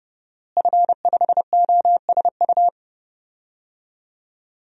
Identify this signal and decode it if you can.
Morse code, heard as a keyed tone, about 30 wpm, 700 Hz. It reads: F5OSU
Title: DXCC-100 Call Signs - 5 Characters - Repeated 2x - 30wpm